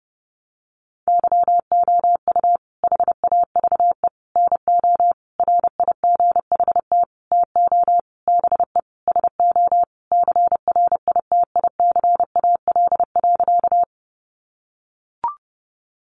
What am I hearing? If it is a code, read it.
Morse code, heard as a keyed tone, about 30 wpm, 700 Hz. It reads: YOU HAVE NO RIGHT TO BE SO CRITICAL.
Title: Sentences from Top 1100 Words 30wpm